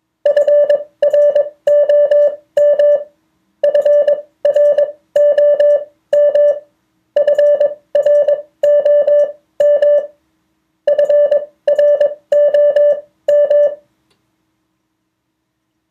Morse code, heard as a keyed tone, about 21 wpm, spaced as for 16 wpm, 580 Hz. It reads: FROM FROM FROM FROM
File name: 2of4-words-1